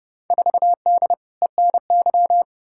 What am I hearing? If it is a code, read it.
Morse code, heard as a keyed tone, about 29 wpm, 700 Hz. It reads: 4D ENY
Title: Field Day Exchanges Encore - 30wpm